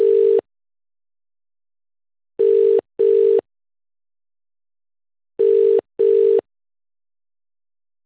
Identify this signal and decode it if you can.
Ringback tone (UK), 400 + 450 Hz.